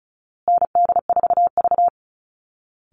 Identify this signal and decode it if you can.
Morse code, heard as a keyed tone, about 34 wpm, 700 Hz. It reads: ND4V
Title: Call Signs Repeated 2x - 35wpm